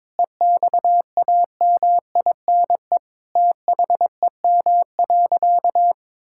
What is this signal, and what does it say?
Morse code, heard as a keyed tone, about 21 wpm, 700 Hz. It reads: EXAMINE THEM.
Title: Sentences from Top 1600 Words 22wpm